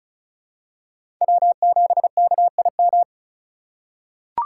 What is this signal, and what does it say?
Morse code, heard as a keyed tone, about 36 wpm, 700 Hz. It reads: W7KIM
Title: USA Call Signs - 2x Character Spacing - 35wpm